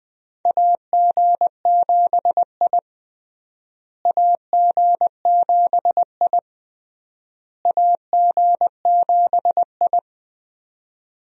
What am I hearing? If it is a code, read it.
Morse code, heard as a keyed tone, about 20 wpm, 700 Hz. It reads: AG7I AG7I AG7I
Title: USA Call Signs - 4 Characters - Repeated 3x - 20wpm